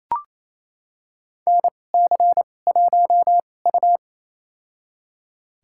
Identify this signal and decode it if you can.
Morse code, heard as a keyed tone, about 28 wpm, spaced as for 20 wpm, 700 Hz. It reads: NC1U
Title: USA Call Signs - 4 Characters - 2x Character Spacing - 28wpm